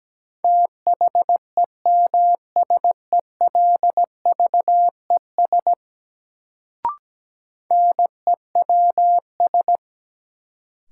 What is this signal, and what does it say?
Morse code, heard as a keyed tone, about 17 wpm, 700 Hz. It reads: THEMSELVES NEWS
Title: Sentences from Top 800 Words - New Word Drill - 17wpm